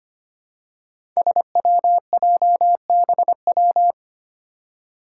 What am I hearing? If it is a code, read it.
Morse code, heard as a keyed tone, about 25 wpm, 700 Hz. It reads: SWJBW
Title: Five Challenging Letter-Number Combinations 25wpm